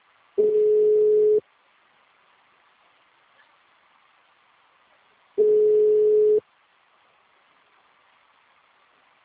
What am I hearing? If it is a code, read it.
Ringback tone (Europe, 425 Hz).